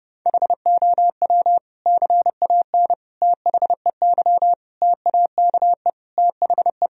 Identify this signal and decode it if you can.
Morse code, heard as a keyed tone, about 30 wpm, 700 Hz. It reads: HOW CAN THEY TAKE THE
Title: Sentences from Top 1000 Words - Review Part 3 - 30wpm